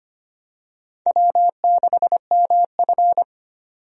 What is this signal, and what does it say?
Morse code, heard as a keyed tone, about 25 wpm, 700 Hz. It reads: W6MF
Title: USA Call Signs - 4 Characters - 25wpm